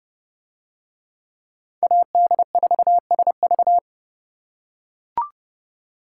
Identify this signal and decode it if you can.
Morse code, heard as a keyed tone, about 30 wpm, 700 Hz. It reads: AD4SV